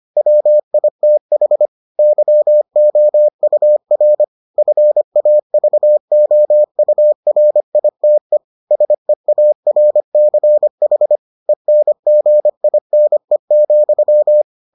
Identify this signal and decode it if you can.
Morse code, heard as a keyed tone, about 25 wpm, 600 Hz. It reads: WITH YOUR FAVOURITE SEARCH ENGINE,